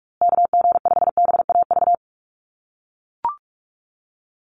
Morse code, heard as a keyed tone, about 44 wpm, 700 Hz. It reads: KG5BAV
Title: USA Call Signs Repeated 2x 45wpm